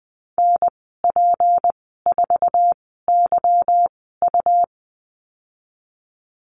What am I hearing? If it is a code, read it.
Morse code, heard as a keyed tone, about 19 wpm, spaced as for 14 wpm, 700 Hz. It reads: NP4YU
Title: 179 Lesson 29g - Slash - Call Signs 20wpm